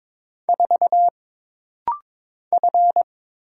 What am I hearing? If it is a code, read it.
Morse code, heard as a keyed tone, about 22 wpm, 700 Hz. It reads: 4 F
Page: This rapid-fire timing is geared to push your ICR (Instant Character Recognition) skills.